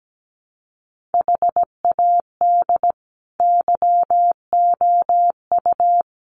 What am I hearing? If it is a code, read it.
Morse code, heard as a keyed tone, about 17 wpm, 700 Hz. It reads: HAD YOU
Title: Sentences from Top 1500 Words 17wpm